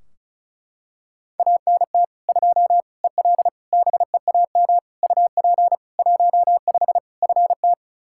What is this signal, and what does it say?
Morse code, heard as a keyed tone, about 34 wpm, 700 Hz. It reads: ANT 2 EL BEAM UP 15 FT
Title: Practice QSO - Exchange 2 - Info - 35wpm